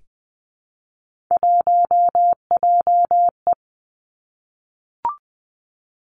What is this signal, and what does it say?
Morse code, heard as a keyed tone, about 20 wpm, 700 Hz. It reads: 1JE
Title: Three Challenging Letter-Number Combinations 20wpm